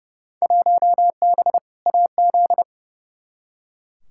Morse code, heard as a keyed tone, about 30 wpm, 700 Hz. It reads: 1B AZ
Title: Field Day Exchanges - 30wpm